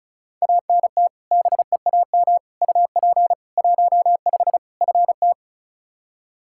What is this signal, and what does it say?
Morse code, heard as a keyed tone, about 34 wpm, 700 Hz. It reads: ANT BEAM UP 15 FT
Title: Practice QSO - Exchange 2 - Info - 35wpm